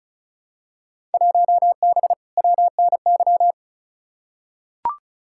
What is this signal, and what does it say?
Morse code, heard as a keyed tone, about 36 wpm, 700 Hz. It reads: 1B WNY
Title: Field Day Exchanges - 35wpm